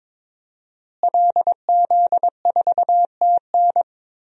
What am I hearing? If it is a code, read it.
Morse code, heard as a keyed tone, about 22 wpm, 700 Hz. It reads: LZ4TN
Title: DXCC-100 Call Signs - 5 Characters - Repeated 3x - 22wpm